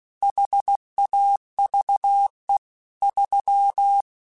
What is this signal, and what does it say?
Morse code, heard as a keyed tone, about 16 wpm, 800 Hz. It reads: HAVE 3
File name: Technical Subjects / Technical Subjects-mp3s / AND Gate @15